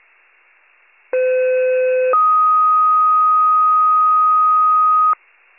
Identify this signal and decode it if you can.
Two-tone page, heard as a one-second tone, then a three-second tone, about 510 then 1250 Hz. It is heard from 1.1 to 5.1 s.